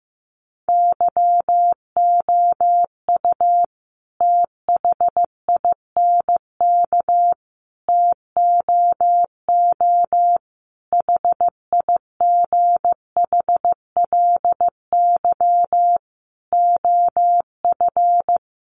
Morse code, heard as a keyed tone, about 15 wpm, 700 Hz. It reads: YOU THINK TOO HIGHLY OF
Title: Sentences from Top 1100 Words 15wpm